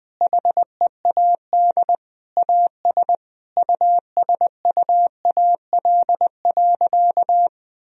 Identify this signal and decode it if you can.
Morse code, heard as a keyed tone, about 20 wpm, 700 Hz. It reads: HEAD AS USUAL.
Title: Sentences from Top 1000 Words - Review Part 2 - 20wpm